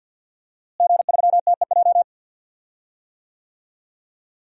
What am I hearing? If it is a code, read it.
Morse code, heard as a keyed tone, about 48 wpm, 700 Hz. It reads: G3TEJ
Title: DXCC-100 Call Signs - 5 Characters - Repeated 3x - 50wpm